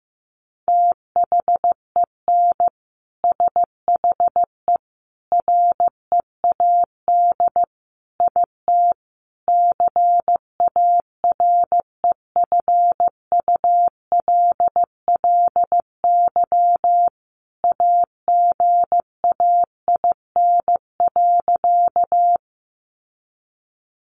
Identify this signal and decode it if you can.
Morse code, heard as a keyed tone, about 15 wpm, 700 Hz. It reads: THEN SHE READ IT CAREFULLY AGAIN.
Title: Sentences from Top 1300 Words 15wpm